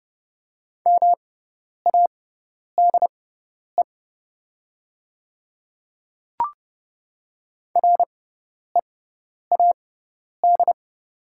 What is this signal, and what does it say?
Morse code, heard as a keyed tone, about 31 wpm, spaced as for 10 wpm, 700 Hz. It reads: MADE READ
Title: 6x Character Spacing - Top 300 Words - 30wpm